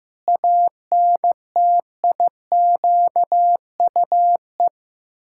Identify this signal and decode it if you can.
Morse code, heard as a keyed tone, about 15 wpm, 700 Hz. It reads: ANTIQUE
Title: Pangram Challenge v2 - 15wpm